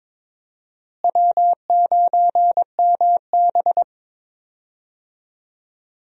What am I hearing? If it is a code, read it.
Morse code, heard as a keyed tone, about 22 wpm, 700 Hz. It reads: W9MB
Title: USA Call Signs - 4 Characters - Repeated 3x - 22wpm